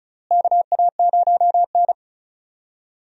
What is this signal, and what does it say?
Morse code, heard as a keyed tone, about 36 wpm, 700 Hz. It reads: KA0N
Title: USA Call Signs - 4 Characters - Repeated 3x - 35wpm